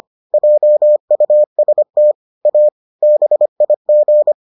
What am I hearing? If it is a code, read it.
Morse code, heard as a keyed tone, about 25 wpm, 600 Hz. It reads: JUST A BIG